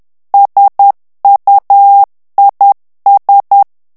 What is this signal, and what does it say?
Morse code, heard as a keyed tone, about 10 wpm, 800 Hz. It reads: SUIS
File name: FRANCAIS TEXTE CLAIR 2 VITESSE 10 MOTS MINUTES